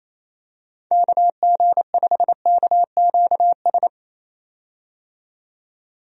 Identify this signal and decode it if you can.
Morse code, heard as a keyed tone, about 27 wpm, 700 Hz. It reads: KG5KQS
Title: USA Call Signs - 6 Characters - 28wpm